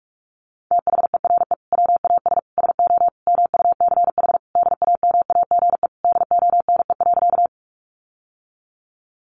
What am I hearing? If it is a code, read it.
Morse code, heard as a keyed tone, about 44 wpm, 700 Hz. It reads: THERE WAS SO MUCH DAMAGE DONE.
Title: Sentences from Top 1100 Words 45wpm